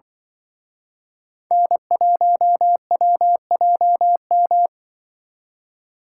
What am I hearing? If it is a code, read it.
Morse code, heard as a keyed tone, about 23 wpm, 700 Hz. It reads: N1WJM
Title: Speed Racing - USA Call Signs 20wpm